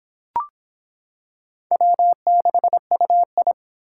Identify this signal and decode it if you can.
Morse code, heard as a keyed tone, about 25 wpm, 700 Hz. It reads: W6UI